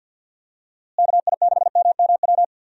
Morse code, heard as a keyed tone, about 52 wpm, 700 Hz. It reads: KI6MMW
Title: USA Call Signs - 6 Characters - Repeated 2x - 50wpm